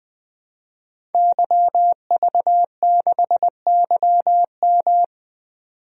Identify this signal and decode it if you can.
Morse code, heard as a keyed tone, about 20 wpm, 700 Hz. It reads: YV6YM
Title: DXCC-100 Call Signs - 5 Characters - 20wpm